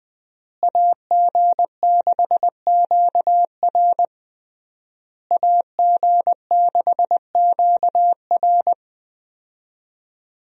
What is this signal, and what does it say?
Morse code, heard as a keyed tone, about 20 wpm, 700 Hz. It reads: AG6QR AG6QR
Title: USA Call Signs - 5 Characters - Repeated 3x - 20wpm